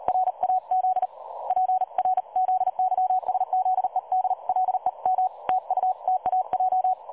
Morse code, heard as a keyed tone, about 36 wpm, 740 Hz. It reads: RAZ PRZYSZEDLEM TUTAJ
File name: dlazaawansowanych / nagraniazpasma / websdr recording 3565.0kHz